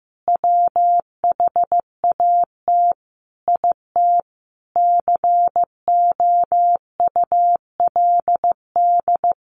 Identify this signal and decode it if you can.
Morse code, heard as a keyed tone, about 15 wpm, 700 Hz. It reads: WHAT IT COULD
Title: Sentences from Top 1500 Words 15wpm